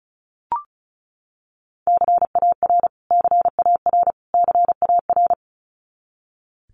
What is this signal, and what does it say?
Morse code, heard as a keyed tone, about 34 wpm, 700 Hz. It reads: CAR CAR CAR
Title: Top 300 Words Repeated 35wpm